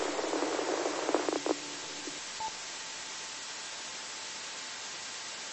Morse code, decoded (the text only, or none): none